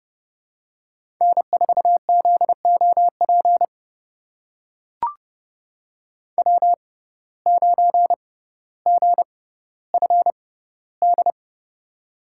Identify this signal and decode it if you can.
Morse code, heard as a keyed tone, about 30 wpm, spaced as for 15 wpm, 700 Hz. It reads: N4ZOP W9GFD
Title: USA Call Signs - 6x Character Spacing - 30wpm